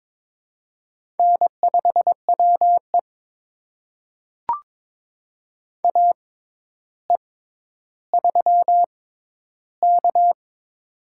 Morse code, heard as a keyed tone, about 22 wpm, spaced as for 11 wpm, 700 Hz. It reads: N5WE AE3K
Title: USA Call Signs - 4 Characters - 6x Character Spacing - 22wpm